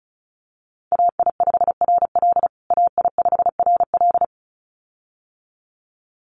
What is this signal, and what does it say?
Morse code, heard as a keyed tone, about 34 wpm, 700 Hz. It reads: AI5RL AI5RL
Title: USA Call Signs Repeated 2x 35wpm